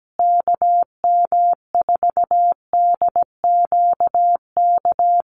Morse code, heard as a keyed tone, about 17 wpm, 700 Hz. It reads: KM4DQK
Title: USA Call Signs Repeated 2x 17wpm